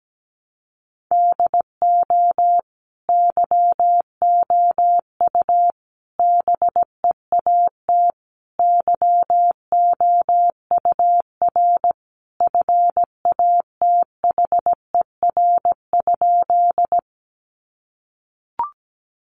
Morse code, heard as a keyed tone, about 17 wpm, 700 Hz. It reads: DO YOU BEAT YOUR FATHER?
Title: Sentences from Top 1300 Words 17wpm